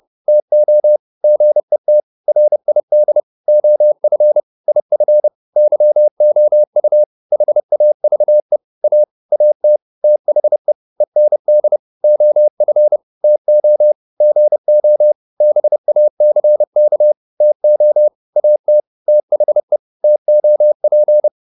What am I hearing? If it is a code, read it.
Morse code, heard as a keyed tone, about 30 wpm, 600 Hz. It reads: TO GET RID OF IF YOU HAVE A AT THE END OF TO GO BACK TO AT THE TOP